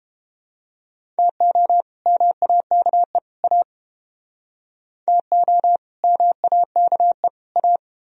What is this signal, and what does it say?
Morse code, heard as a keyed tone, about 31 wpm, 700 Hz. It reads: TO MAKE A TO MAKE A